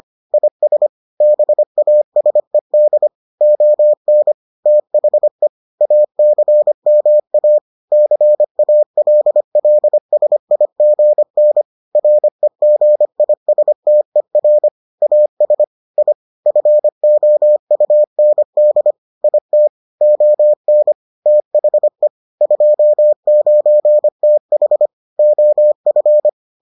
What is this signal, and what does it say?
Morse code, heard as a keyed tone, about 25 wpm, 600 Hz. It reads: IS BASED ON THE ACMA CALLSIGN REGISTER AS I FOUND IT ON THE 29TH OF